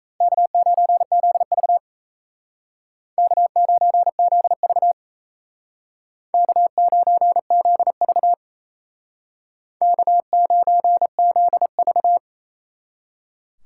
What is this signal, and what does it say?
Morse code, heard as a keyed tone, about 35 wpm, 700 Hz. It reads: K9ZV K9ZV K9ZV K9ZV